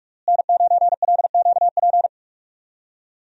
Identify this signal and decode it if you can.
Morse code, heard as a keyed tone, about 41 wpm, 700 Hz. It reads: N9LQP
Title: USA Call Signs - 5 Characters - Repeated 3x - 45wpm